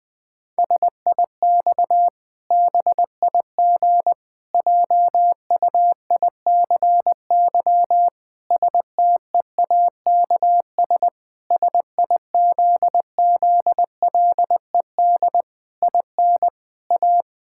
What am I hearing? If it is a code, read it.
Morse code, heard as a keyed tone, about 20 wpm, 700 Hz. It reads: SIX BIG JUICY STEAKS SIZZLED IN A
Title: Pangram Challenge v2 - 20wpm